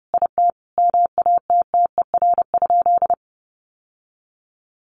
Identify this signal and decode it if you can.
Morse code, heard as a keyed tone, about 30 wpm, 700 Hz. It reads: IT MATTER?
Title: Sentences from Top 700 Words 30wpm